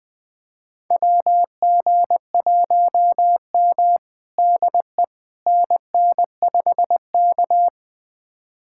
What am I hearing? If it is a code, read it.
Morse code, heard as a keyed tone, about 20 wpm, 700 Hz. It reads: WG1M DE NN5K
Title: USA Callsign DE Callsign - 4 Characters - 20wpm